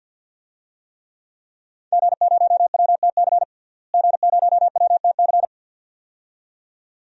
Morse code, heard as a keyed tone, about 52 wpm, 700 Hz. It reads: G0WTC G0WTC
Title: DXCC-100 Call Signs - 5 Characters - Repeated 2x - 50wpm